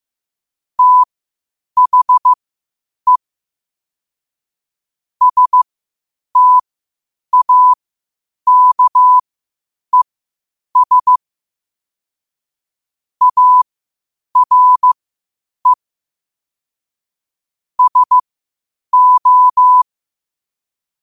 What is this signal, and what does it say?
Morse code, heard as a keyed tone, about 15 wpm, spaced as for 8 wpm, 1000 Hz. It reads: THE STAKES ARE SO
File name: qotdfiles / 8wpm